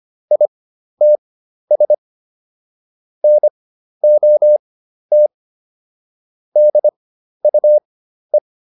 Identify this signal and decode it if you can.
Morse code, heard as a keyed tone, about 25 wpm, spaced as for 12 wpm, 600 Hz. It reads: ITS NOT DUE